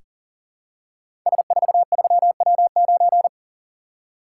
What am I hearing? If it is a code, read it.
Morse code, heard as a keyed tone, about 41 wpm, 700 Hz. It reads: S43W9
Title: Code Groups - 5 Letter:Numbers - 40wpm